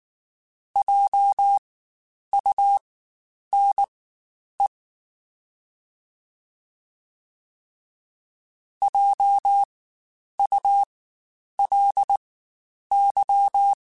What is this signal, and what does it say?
Morse code, heard as a keyed tone, about 19 wpm, spaced as for 9 wpm, 800 Hz. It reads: JUNE JULY
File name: Spaced Alphas / Spaced Alphas mp3s / Js sp @18